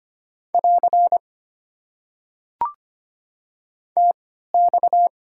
Morse code, heard as a keyed tone, about 25 wpm, spaced as for 14 wpm, 700 Hz. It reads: <AR> TX
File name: Top 90 QSO Elements - 3x Character Spacing - 25wpm